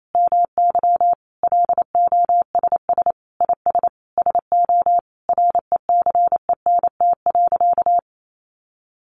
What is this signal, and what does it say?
Morse code, heard as a keyed tone, about 28 wpm, 700 Hz. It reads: MY LOSS IS SO RECENT.
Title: Sentences from Top 1400 Words 28wpm